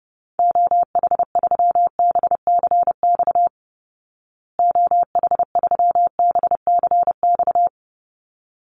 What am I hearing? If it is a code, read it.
Morse code, heard as a keyed tone, about 30 wpm, 700 Hz. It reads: OH3BCX OH3BCX
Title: Call Signs Repeated 3x - 30wpm